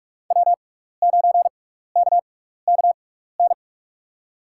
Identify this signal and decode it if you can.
Morse code, heard as a keyed tone, about 45 wpm, spaced as for 15 wpm, 700 Hz. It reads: W9KKN
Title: USA Call Signs - 5 Characters - 6x Character Spacing - 45wpm